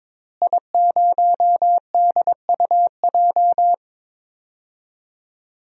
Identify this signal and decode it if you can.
Morse code, heard as a keyed tone, about 22 wpm, 700 Hz. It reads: I0DUJ
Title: DXCC-100 Call Signs - 5 Characters - Repeated 2x - 22wpm